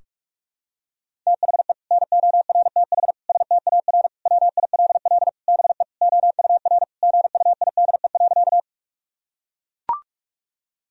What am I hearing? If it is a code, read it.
Morse code, heard as a keyed tone, about 41 wpm, 700 Hz. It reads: THE NORTH STAR WILL BE OUR GUIDE.
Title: Sentences from Top 1000 Words - Review Part 3 - 45wpm